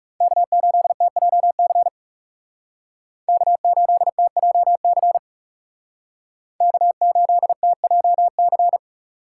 Morse code, heard as a keyed tone, about 36 wpm, 700 Hz. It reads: K8TJC K8TJC K8TJC
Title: Speed Racing - USA Call Signs 30wpm